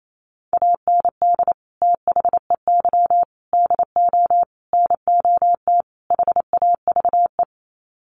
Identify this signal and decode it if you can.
Morse code, heard as a keyed tone, about 28 wpm, 700 Hz. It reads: AND THEY DO NOT HAVE
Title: Sets of 5 Words Encore2 28wpm